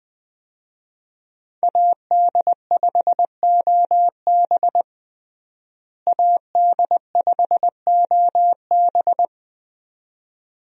Morse code, heard as a keyed tone, about 20 wpm, 700 Hz. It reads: AD5OB AD5OB
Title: USA Call Signs - 5 Characters - Repeated 2x - 20wpm